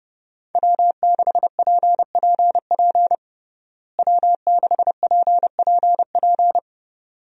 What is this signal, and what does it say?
Morse code, heard as a keyed tone, about 30 wpm, 700 Hz. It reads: W6PPP W6PPP
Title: USA Call Signs - 5 Characters - Repeated 2x - 30wpm